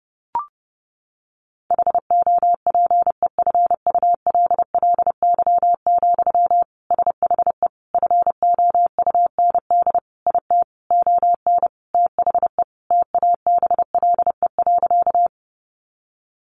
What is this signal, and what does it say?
Morse code, heard as a keyed tone, about 28 wpm, 700 Hz. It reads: HOPEFULLY, SHE FOUND IT ON THE TABLE.
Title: Sentences from Top 1500 Words 30wpm